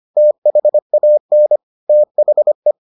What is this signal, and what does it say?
Morse code, heard as a keyed tone, about 25 wpm, 600 Hz. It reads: THAN THE